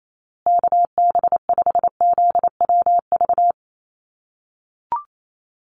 Morse code, heard as a keyed tone, about 28 wpm, 700 Hz. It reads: KB5ZWV
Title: USA Call Signs Repeated 3x 28wpm